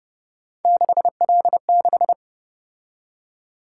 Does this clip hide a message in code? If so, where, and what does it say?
Yes: Morse code from 0.7 to 2.1 s, heard as a keyed tone, about 29 wpm, 700 Hz. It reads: 6L6